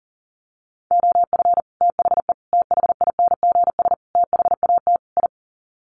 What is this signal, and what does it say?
Morse code, heard as a keyed tone, about 37 wpm, 700 Hz. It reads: OF THE THINGS THAT I
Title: Sets of 5 Words Encore 40wpm